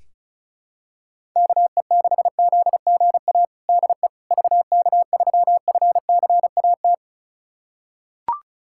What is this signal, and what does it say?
Morse code, heard as a keyed tone, about 36 wpm, 700 Hz. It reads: KE6ZGA DE VK3FCAT
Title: USA Callsign DE Callsign 35wpm